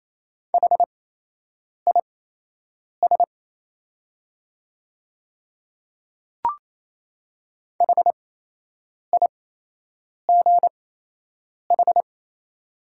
Morse code, heard as a keyed tone, about 28 wpm, spaced as for 8 wpm, 690 Hz. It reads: HIS HIGH
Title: 8x Character Spacing - Top 200 Words - 28wpm